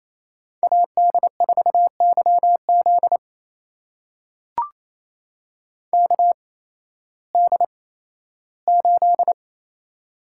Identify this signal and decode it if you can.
Morse code, heard as a keyed tone, about 28 wpm, spaced as for 15 wpm, 700 Hz. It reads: AD4YZ KD8
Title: USA Call Signs - 5 Characters - 8x Character Spacing - 28wpm